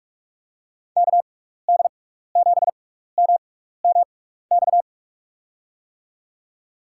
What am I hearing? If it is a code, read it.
Morse code, heard as a keyed tone, about 48 wpm, spaced as for 16 wpm, 700 Hz. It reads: KD7MMX
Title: USA Call Signs - 6 Characters - 6x Character Spacing - 45wpm